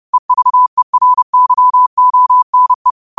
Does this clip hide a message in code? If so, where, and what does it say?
Yes: Morse code from 0.1 to 2.9 s, heard as a keyed tone, about 30 wpm, 1000 Hz. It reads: EVERYONE